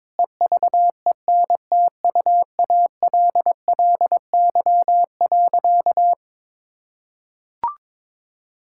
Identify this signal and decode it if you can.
Morse code, heard as a keyed tone, about 22 wpm, 700 Hz. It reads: EVENTUALLY.
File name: practice / 22wpm / Sentences with 8-Word Phrases - 22wpm